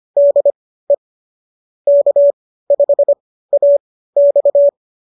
Morse code, heard as a keyed tone, about 25 wpm, spaced as for 15 wpm, 570 Hz. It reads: DE K5AX